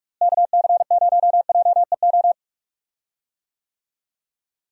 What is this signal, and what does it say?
Morse code, heard as a keyed tone, about 46 wpm, 700 Hz. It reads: KC0JEO
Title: USA Call Signs - 6 Characters - Repeated 3x - 45wpm